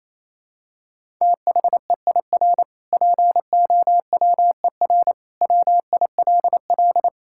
Morse code, heard as a keyed tone, about 28 wpm, 700 Hz. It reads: THEIR POWER WILL
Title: Sentences from Top 1000 Words - Review Part 6 - 28wpm